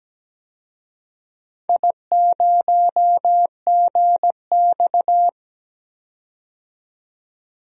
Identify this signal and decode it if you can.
Morse code, heard as a keyed tone, about 17 wpm, 700 Hz. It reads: I0GX
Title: DXCC-100 Call Signs - 4 Characters - 17wpm